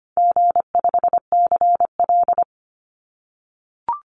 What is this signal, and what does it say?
Morse code, heard as a keyed tone, about 25 wpm, 700 Hz. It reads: G5CL
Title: Easy Call Signs - 25wpm